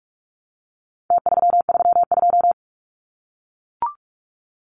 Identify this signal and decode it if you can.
Morse code, heard as a keyed tone, about 44 wpm, 700 Hz. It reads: T332
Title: Numbers - Contest Style - 4-Digit 45wpm